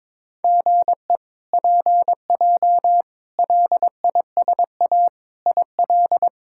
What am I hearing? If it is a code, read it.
Morse code, heard as a keyed tone, about 21 wpm, 700 Hz. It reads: GE PJ LISA IL
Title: K1USN SST QSO Practice - 22wpm